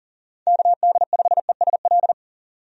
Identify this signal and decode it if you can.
Morse code, heard as a keyed tone, about 40 wpm, 700 Hz. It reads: KD5ESL
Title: USA Call Signs - 6 Characters - 40wpm